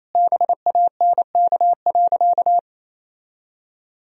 Morse code, heard as a keyed tone, about 28 wpm, 700 Hz. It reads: BANK.
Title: Sentences from Top 1000 Words - Review Part 4 - 28wpm